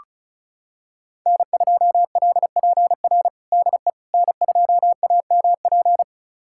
Morse code, heard as a keyed tone, about 36 wpm, 700 Hz. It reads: N2LPR DE N2AMP
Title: USA Callsign DE Callsign - 5 Characters - 35wpm